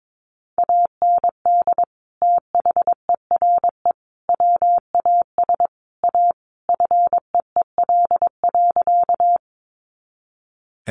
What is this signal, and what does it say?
Morse code, heard as a keyed tone, about 22 wpm, 700 Hz. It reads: AND THERE WAS A FEEL.
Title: A Sound Of Thunder 22wpm